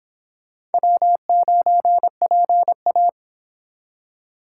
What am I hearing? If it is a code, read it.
Morse code, heard as a keyed tone, about 29 wpm, 700 Hz. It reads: W9PA